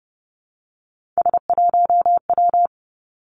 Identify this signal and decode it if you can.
Morse code, heard as a keyed tone, about 30 wpm, 700 Hz. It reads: S1W
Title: Three Challenging Letter-Number Combinations 30wpm